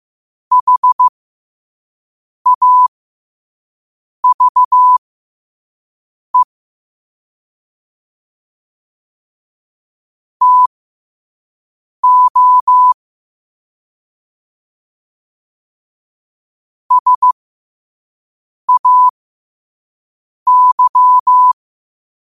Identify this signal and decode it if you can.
Morse code, heard as a keyed tone, about 15 wpm, spaced as for 5 wpm, 1000 Hz. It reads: HAVE TO SAY
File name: qotdfiles / 5wpm